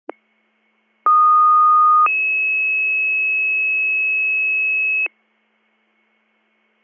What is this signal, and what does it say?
Two-tone page, heard as a one-second tone, then a three-second tone, about 1230 then 2600 Hz. It is heard from 1.1 to 5.1 s.